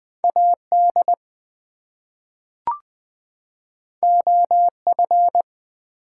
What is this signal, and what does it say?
Morse code, heard as a keyed tone, about 20 wpm, 700 Hz. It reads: AD OF
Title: Two Letter Words - 20wpm